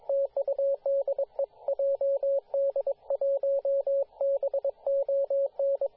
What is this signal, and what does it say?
Morse code, heard as a keyed tone, about 22 wpm, 550 Hz, with radio noise. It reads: TUDE JD1BON
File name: posts / 2017-sep-22-0800z-jd1bon-14-040mhz-cw